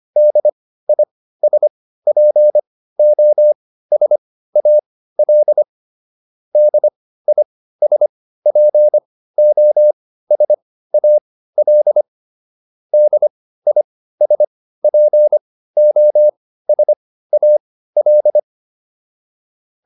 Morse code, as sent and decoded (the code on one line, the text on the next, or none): -.. .. ... .--. --- ... .- .-.. / -.. .. ... .--. --- ... .- .-.. / -.. .. ... .--. --- ... .- .-..
DISPOSAL DISPOSAL DISPOSAL